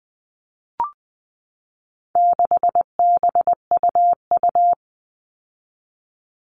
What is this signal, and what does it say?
Morse code, heard as a keyed tone, about 20 wpm, 700 Hz. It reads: 6BUU
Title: All Cut Numbers - 4-Digit 20wpm